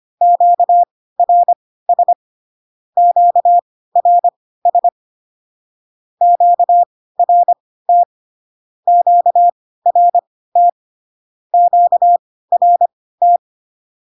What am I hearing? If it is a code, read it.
Morse code, heard as a keyed tone, about 25 wpm, spaced as for 16 wpm, 700 Hz. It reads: QRS QRS QRT QRT QRT
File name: cw / Q-codes most common repeated 3 times 10 25wpm ik1hge